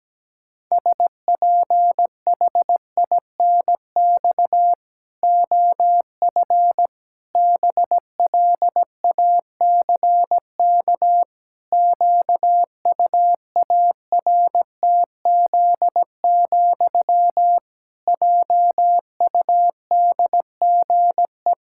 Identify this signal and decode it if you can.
Morse code, heard as a keyed tone, about 17 wpm, 700 Hz. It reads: SPHINX OF BLACK QUARTZ, JUDGE
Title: Pangram Challenge v2 - 17wpm